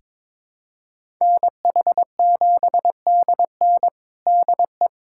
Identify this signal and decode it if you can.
Morse code, heard as a keyed tone, about 22 wpm, 700 Hz. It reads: NH7DN DE
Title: USA Callsign DE Callsign 22wpm